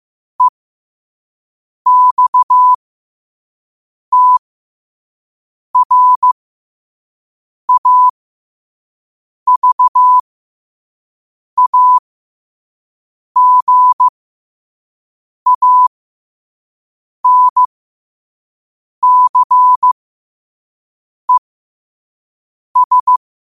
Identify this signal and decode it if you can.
Morse code, heard as a keyed tone, about 15 wpm, spaced as for 5 wpm, 1000 Hz. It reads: EXTRAVAGANCES
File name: qotdfiles / 5wpm